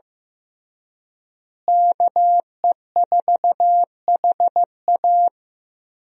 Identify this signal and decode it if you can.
Morse code, heard as a keyed tone, about 15 wpm, 700 Hz. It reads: KE4HA
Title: USA Call Signs - 5 Characters - Repeated 3x - 15wpm